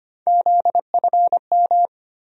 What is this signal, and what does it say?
Morse code, heard as a keyed tone, about 25 wpm, 700 Hz. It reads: ZFM
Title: USA Call Sign Postfixes - 3 Letters 25wpm